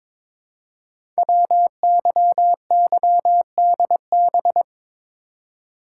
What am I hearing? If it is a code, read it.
Morse code, heard as a keyed tone, about 23 wpm, 700 Hz. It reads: WYYDB
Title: Code Groups - 5 Letters - 22wpm